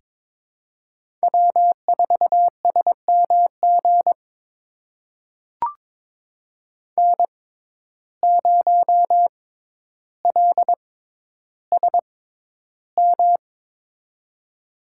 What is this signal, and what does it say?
Morse code, heard as a keyed tone, about 23 wpm, spaced as for 11 wpm, 700 Hz. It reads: W4SMG N0LSM
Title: USA Call Signs - 5 Characters - 6x Character Spacing - 22wpm